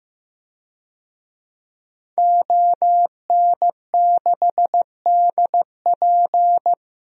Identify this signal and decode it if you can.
Morse code, heard as a keyed tone, about 15 wpm, 700 Hz. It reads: ON6DP
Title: DXCC-100 Call Signs - 5 Characters - 15wpm